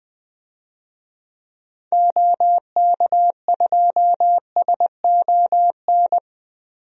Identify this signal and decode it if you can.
Morse code, heard as a keyed tone, about 20 wpm, 700 Hz. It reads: OK2SON
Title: DXCC-100 Call Signs - 6 Characters - 20wpm